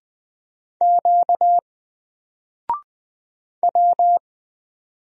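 Morse code, heard as a keyed tone, about 20 wpm, 700 Hz. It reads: Q W
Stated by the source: Single letter and number Morse drill at 20 wpm, from N4LCD.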